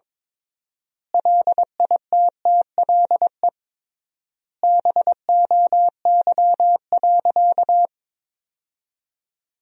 Morse code, heard as a keyed tone, about 22 wpm, 700 Hz. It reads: LITTLE BOY.
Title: 3x Word-Spacing - Sentences from Top 200 Words - 22wpm